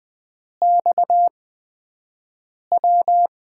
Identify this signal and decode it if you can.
Morse code, heard as a keyed tone, about 20 wpm, spaced as for 5 wpm, 700 Hz. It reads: XW
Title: USA Call Signs - 5 Characters - 8x Character Spacing - 20wpm